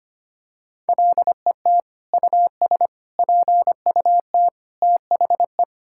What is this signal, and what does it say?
Morse code, heard as a keyed tone, about 25 wpm, 700 Hz. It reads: LET US PUT THE
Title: Sentences from Top 1000 Words - Review Part 5 - 25wpm